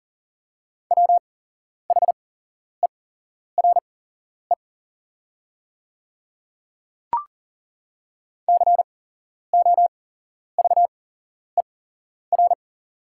Morse code, heard as a keyed tone, about 40 wpm, spaced as for 11 wpm, 690 Hz. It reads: WHERE COVER
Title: 8x Character Spacing - Top 500 Words - 40wpm